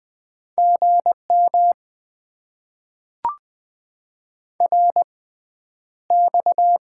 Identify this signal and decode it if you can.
Morse code, heard as a keyed tone, about 20 wpm, spaced as for 10 wpm, 700 Hz. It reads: GM RX
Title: Top 90 QSO Elements - 6x Character Spacing - 20wpm